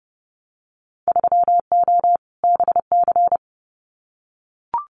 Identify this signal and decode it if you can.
Morse code, heard as a keyed tone, about 30 wpm, 700 Hz. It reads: 3O BC
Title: Winter Field Day Exchanges Encore 30wpm